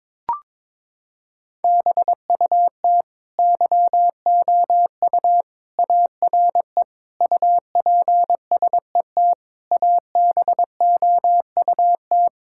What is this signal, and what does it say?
Morse code, heard as a keyed tone, about 22 wpm, 700 Hz. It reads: BUT YOU ARE UPSET ABOUT
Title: Sentences from Top 1000 Words - Review Part 5 - 22wpm